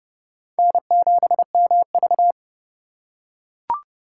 Morse code, heard as a keyed tone, about 27 wpm, 700 Hz. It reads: N7MV